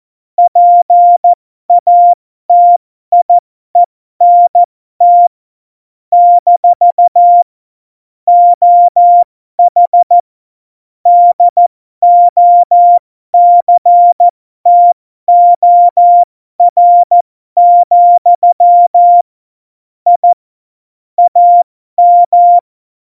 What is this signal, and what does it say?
Morse code, heard as a keyed tone, about 14 wpm, spaced as for 12 wpm, 700 Hz. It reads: PATIENT - OH DOCTOR, I AM
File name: Morsekurs / Witze